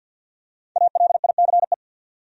Morse code, heard as a keyed tone, about 51 wpm, 700 Hz. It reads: ALICE
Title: USA Common Names - 50wpm